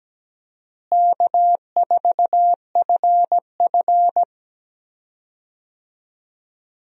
Morse code, heard as a keyed tone, about 17 wpm, 700 Hz. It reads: K4FF
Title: USA Call Signs - 4 Characters - Repeated 3x - 17wpm